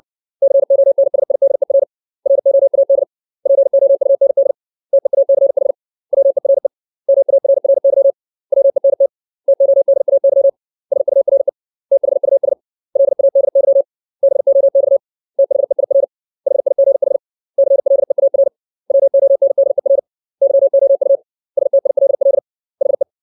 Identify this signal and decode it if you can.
Morse code, heard as a keyed tone, about 60 wpm, 550 Hz. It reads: CONSIDER ROAD POUND TEACH WERE MARRY WENT TODAY SURE THUS LADY BOX THEIR HIGH CLEAR WONDER YOU STILL HE
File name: 632 words / 632 Words 60WPM